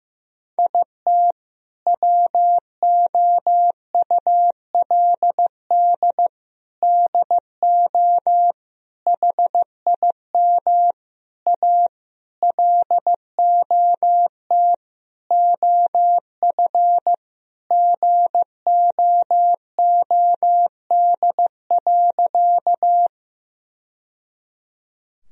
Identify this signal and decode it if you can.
Morse code, heard as a keyed tone, about 15 wpm, 700 Hz. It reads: IT WOULD DO HIM A LOT OF GOOD.
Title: Sentences from Top 1000 Words - Review Part 3 - 15wpm